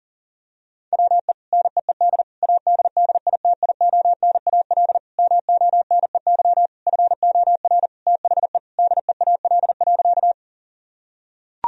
Morse code, heard as a keyed tone, about 40 wpm, 700 Hz. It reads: WE NEED ADDITIONAL MONEY FOR THE DEAL.
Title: Sentences from Top 1000 Words - Review Part 4 - 40wpm